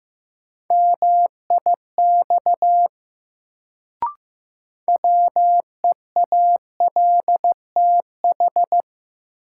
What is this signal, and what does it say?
Morse code, heard as a keyed tone, about 15 wpm, 700 Hz. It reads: MIX WEALTH
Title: Sentences from Top 1000 Words - Review Part 4 - New Word Drill - 15wpm